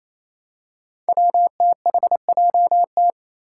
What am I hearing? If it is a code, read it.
Morse code, heard as a keyed tone, about 28 wpm, 700 Hz. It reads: WTHJT
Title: Five Challenging Letter-Number Combinations 28wpm